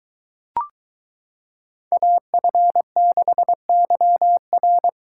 Morse code, heard as a keyed tone, about 22 wpm, 700 Hz. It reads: AF6YR